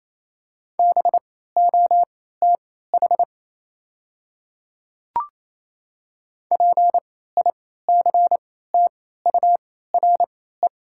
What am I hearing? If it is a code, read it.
Morse code, heard as a keyed tone, about 28 wpm, spaced as for 16 wpm, 700 Hz. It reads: BOTH PICTURE
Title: 3x Character Spacing - Top 300 Words - 28wpm